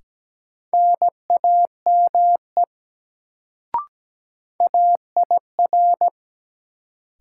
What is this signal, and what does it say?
Morse code, heard as a keyed tone, about 17 wpm, 700 Hz. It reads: NAME AIR
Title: Sentences from Top 200 Words - New Word Drill - 17wpm